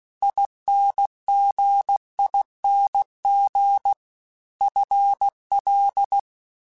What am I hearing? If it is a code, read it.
Morse code, heard as a keyed tone, about 16 wpm, 800 Hz. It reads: INGING FL
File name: Special INGs-mp3s / Special INGs-mp3s / Dble Ings sp @15